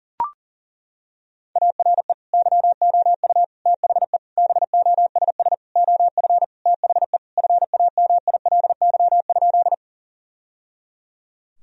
Morse code, heard as a keyed tone, about 40 wpm, 700 Hz. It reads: ARE YOU THE BOSS OF THE FAMILY?
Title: Sentences from Top 1000 Words - Review Part 4 - 40wpm